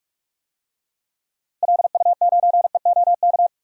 Morse code, heard as a keyed tone, about 44 wpm, 690 Hz. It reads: LU9EOK
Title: DXCC-100 Call Signs - 6 Characters - Repeated 2x - 45wpm